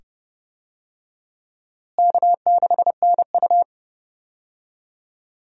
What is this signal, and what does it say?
Morse code, heard as a keyed tone, about 29 wpm, 700 Hz. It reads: K6NU
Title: USA Call Signs - 4 Characters - Repeated 2x - 30wpm